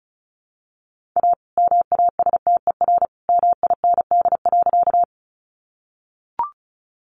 Morse code, heard as a keyed tone, about 34 wpm, 700 Hz. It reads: A MASTER MIND.
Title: Sentences from Top 1300 Words 35wpm